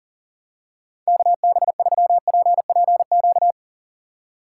Morse code, heard as a keyed tone, about 40 wpm, 700 Hz. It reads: KB3PPQ
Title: USA Call Signs - 6 Characters - 3x Character Spacing - 40wpm